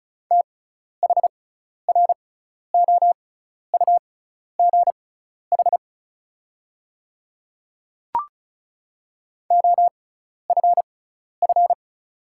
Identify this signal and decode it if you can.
Morse code, heard as a keyed tone, about 36 wpm, spaced as for 12 wpm, 700 Hz. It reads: THROUGH OFF
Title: 6x Character Spacing - Top 200 Words - 35wpm